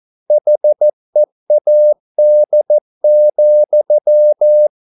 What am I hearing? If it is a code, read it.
Morse code, heard as a keyed tone, about 14 wpm, 600 Hz. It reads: HEAD,